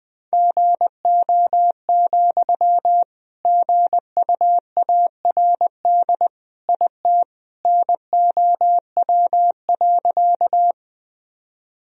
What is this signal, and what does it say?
Morse code, heard as a keyed tone, about 21 wpm, 700 Hz. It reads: GO, GUARD IT NOW.
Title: Sentences from Top 1000 Words - Review Part 4 - 20wpm